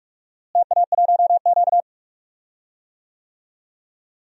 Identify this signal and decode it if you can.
Morse code, heard as a keyed tone, about 46 wpm, 700 Hz. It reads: TA1Q
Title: DXCC-100 Call Signs - 4 Characters - Repeated 2x - 45wpm